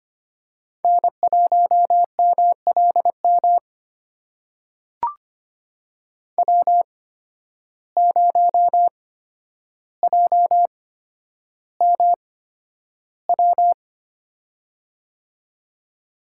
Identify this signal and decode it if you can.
Morse code, heard as a keyed tone, about 25 wpm, spaced as for 11 wpm, 700 Hz. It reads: N1MLM W0JMW
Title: USA Call Signs - 5 Characters - 8x Character Spacing - 25wpm